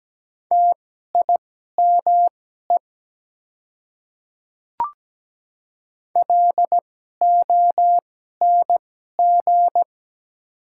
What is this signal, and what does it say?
Morse code, heard as a keyed tone, about 17 wpm, spaced as for 12 wpm, 700 Hz. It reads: TIME LONG
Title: 2x Character Spacing - Top 200 Words - 17wpm